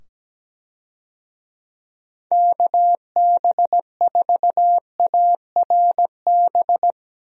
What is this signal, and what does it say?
Morse code, heard as a keyed tone, about 17 wpm, 700 Hz. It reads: KB4ARB
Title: USA Call Signs - 6 Characters - 17wpm